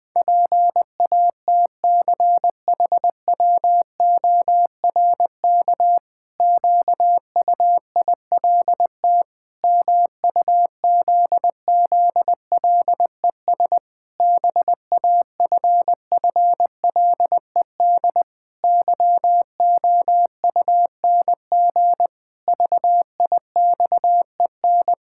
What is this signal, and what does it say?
Morse code, heard as a keyed tone, about 20 wpm, 700 Hz. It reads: PATCHWORK QUILT MUZZLES BAFFLED YOUNG VIXEN
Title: Pangram Challenge v2 - 20wpm